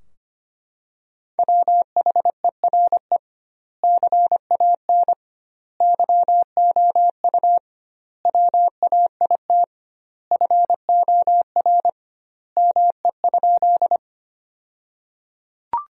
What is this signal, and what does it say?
Morse code, heard as a keyed tone, about 25 wpm, 700 Hz. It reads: WHERE CAN YOU WAIT FOR ME?
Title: 2x Word-Spacing - Sentences from Top 500 Words - 25wpm